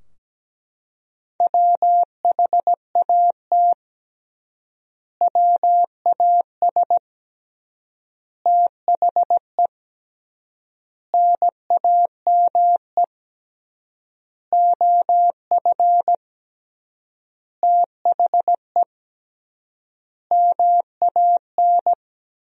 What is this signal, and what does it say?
Morse code, heard as a keyed tone, about 17 wpm, 700 Hz. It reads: WHAT WAS THE NAME OF THE MAN
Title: 3x Word-Spacing - Sentences from Top 300 Words - 17wpm